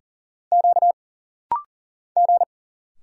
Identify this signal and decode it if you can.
Morse code, heard as a keyed tone, about 40 wpm, 700 Hz. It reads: Q G
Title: Single Letter-Number Alt - Rapid-Fire 40wpm